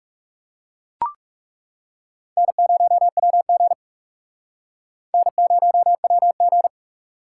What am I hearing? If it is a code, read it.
Morse code, heard as a keyed tone, about 36 wpm, 700 Hz. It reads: N0WG N0WG
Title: Speed Racing - USA 1x2 Call Signs 30wpm